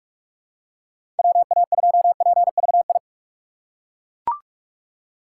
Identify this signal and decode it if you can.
Morse code, heard as a keyed tone, about 42 wpm, 700 Hz. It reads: WA2PVI
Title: USA Call Signs - 6 Characters - 2x Character Spacing - 45wpm